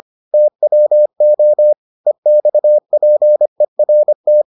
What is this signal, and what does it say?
Morse code, heard as a keyed tone, about 25 wpm, 600 Hz. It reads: TWO EXPERT